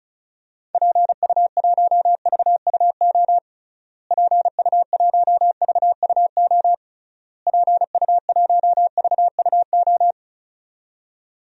Morse code, heard as a keyed tone, about 34 wpm, 700 Hz. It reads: PU1VUO PU1VUO PU1VUO
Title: DXCC-100 Call Signs - 6 Characters - Repeated 3x - 35wpm